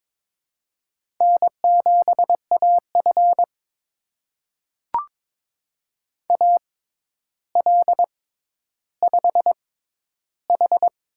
Morse code, heard as a keyed tone, about 22 wpm, spaced as for 11 wpm, 700 Hz. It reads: N7AF AL5H
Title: USA Call Signs - 4 Characters - 6x Character Spacing - 22wpm